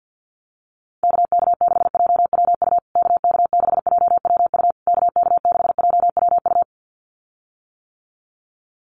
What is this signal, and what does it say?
Morse code, heard as a keyed tone, about 48 wpm, 700 Hz. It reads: KK6JWU KK6JWU KK6JWU
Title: USA Call Signs Repeated 3x 50wpm